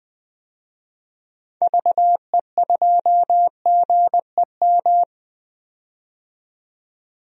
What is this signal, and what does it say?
Morse code, heard as a keyed tone, about 20 wpm, 700 Hz. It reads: VE2GEM
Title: DXCC-100 Call Signs - 6 Characters - 20wpm